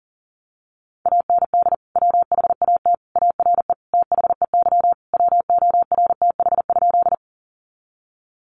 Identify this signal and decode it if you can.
Morse code, heard as a keyed tone, about 40 wpm, 700 Hz. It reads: AND WHAT ARE THEY WORTH?
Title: Sentences from Top 800 Words 40wpm